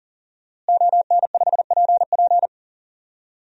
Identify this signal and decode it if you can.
Morse code, heard as a keyed tone, about 40 wpm, 700 Hz. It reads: ON5PP
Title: DXCC-100 Call Signs - 5 Characters - Repeated 2x - 40wpm